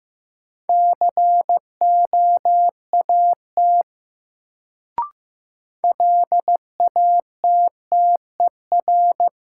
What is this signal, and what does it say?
Morse code, heard as a keyed tone, about 15 wpm, 700 Hz. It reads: COAT LATTER
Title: Sentences from Top 1600 Words - New Word Drill Alt - 15wpm